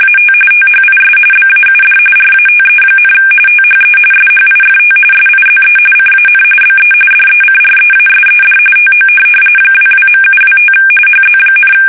RTTY, about 45.45 baud. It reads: AT THIS BAUD RATE, THE SHIFT SIZE CAN BE DECREASED TO 22.725 HZ,
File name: fsk850 / teletype-RTTY